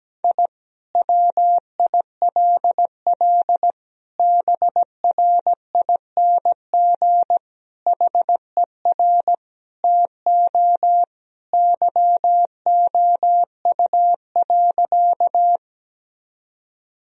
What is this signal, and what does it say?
Morse code, heard as a keyed tone, about 17 wpm, 700 Hz. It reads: I WILL BRING HER TO YOU.
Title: Sentences from Top 1000 Words - Review Part 3 - 17wpm